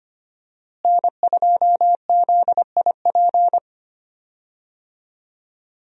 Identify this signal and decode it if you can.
Morse code, heard as a keyed tone, about 25 wpm, 700 Hz. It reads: N2ZIP
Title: USA Call Signs - 5 Characters - Repeated 3x - 25wpm